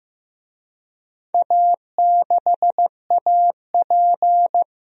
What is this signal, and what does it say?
Morse code, heard as a keyed tone, about 16 wpm, 700 Hz. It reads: A6AP
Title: DXCC-100 Call Signs - 4 Characters - 15wpm